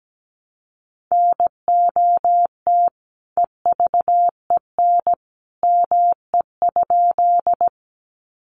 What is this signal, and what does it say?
Morse code, heard as a keyed tone, about 17 wpm, 700 Hz. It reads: NOT EVEN ME?
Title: Sentences from Top 600 Words 17wpm